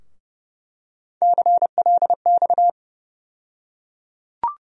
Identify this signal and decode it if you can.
Morse code, heard as a keyed tone, about 30 wpm, 700 Hz. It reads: CLX
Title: USA Call Sign Postfixes - 3 Letters 30wpm